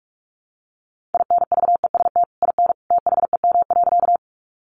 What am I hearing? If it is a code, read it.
Morse code, heard as a keyed tone, about 44 wpm, 700 Hz. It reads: INVEST IN THEM.
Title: Sentences from Top 1100 Words 45wpm